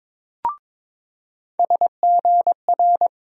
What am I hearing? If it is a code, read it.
Morse code, heard as a keyed tone, about 22 wpm, 700 Hz. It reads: SGR
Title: USA Call Sign Postfixes - 3 Letters 22wpm